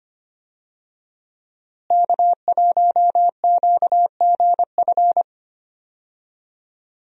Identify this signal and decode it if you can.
Morse code, heard as a keyed tone, about 25 wpm, 700 Hz. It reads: K1QGF
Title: USA Call Signs - 5 Characters - Repeated 3x - 25wpm